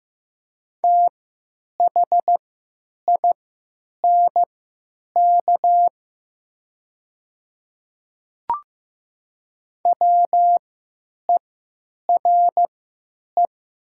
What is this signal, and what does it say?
Morse code, heard as a keyed tone, about 15 wpm, spaced as for 8 wpm, 700 Hz. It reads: THINK WERE
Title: 3x Character Spacing - Top 200 Words - 15wpm